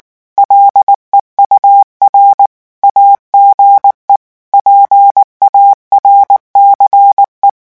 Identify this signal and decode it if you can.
Morse code, heard as a keyed tone, about 19 wpm, 800 Hz. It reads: LEUR AGE PARCE
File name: morse mp3 / revision9 citations humoristiques vitesse 18 mots minute